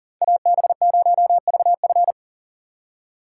Morse code, heard as a keyed tone, about 52 wpm, 700 Hz. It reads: AB0VF